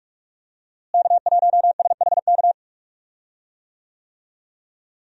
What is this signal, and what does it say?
Morse code, heard as a keyed tone, about 41 wpm, 700 Hz. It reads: K1SHK
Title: USA Call Signs - 5 Characters - Repeated 2x - 45wpm